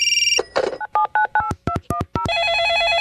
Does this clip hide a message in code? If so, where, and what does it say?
Yes: DTMF from 0.8 to 2.3 s, heard as telephone touch tones, tones of about 100 ms. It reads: 97998210